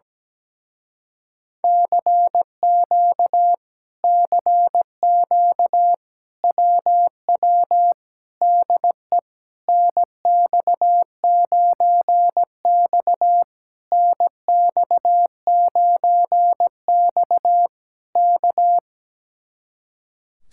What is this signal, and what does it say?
Morse code, heard as a keyed tone, about 17 wpm, 700 Hz. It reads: CQ CQ WW DE NX9X NX9X K
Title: Speed-Racing Common QSO Phrases - 17wpm